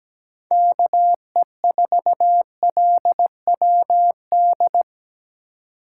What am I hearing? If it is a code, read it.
Morse code, heard as a keyed tone, about 17 wpm, 700 Hz. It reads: KE4LWD
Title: USA Call Signs - 6 Characters - 3x Character Spacing - 17wpm